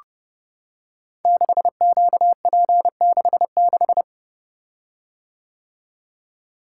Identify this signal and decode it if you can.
Morse code, heard as a keyed tone, about 30 wpm, 700 Hz. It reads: 6QP66
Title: Code Groups - 5 Letter:Numbers - 30wpm